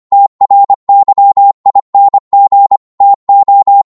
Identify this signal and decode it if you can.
Morse code, heard as a keyed tone, about 25 wpm, 800 Hz. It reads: TRYING TO
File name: 25wpm